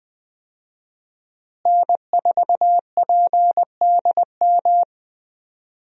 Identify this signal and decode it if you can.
Morse code, heard as a keyed tone, about 20 wpm, 700 Hz. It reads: N4PDM
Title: USA Call Signs - 5 Characters - Repeated 2x - 20wpm